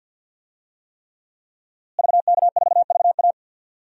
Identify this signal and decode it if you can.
Morse code, heard as a keyed tone, about 52 wpm, 700 Hz. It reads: VK4VA
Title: DXCC-100 Call Signs - 5 Characters - Repeated 3x - 50wpm